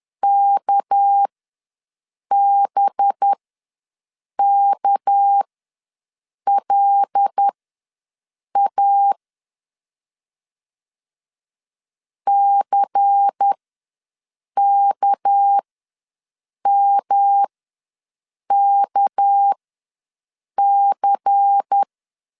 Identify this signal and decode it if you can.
Morse code, heard as a keyed tone, about 10 wpm, spaced as for 6 wpm, 800 Hz. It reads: KBKLA CKMKC
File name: LEZIONI / 6 PKX0 / 6 PK ws 10wpm 3745